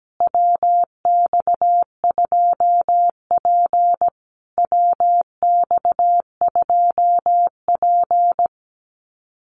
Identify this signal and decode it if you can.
Morse code, heard as a keyed tone, about 17 wpm, 700 Hz. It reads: WX2P WX2P
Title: USA Call Signs Repeated 3x 17wpm